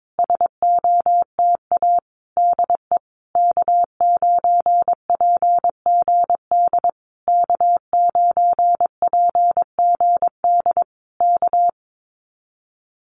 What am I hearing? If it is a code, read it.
Morse code, heard as a keyed tone, about 22 wpm, 700 Hz. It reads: SOTA DE K9PGD K9PGD K
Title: Common QSO Phrases Encore 22wpm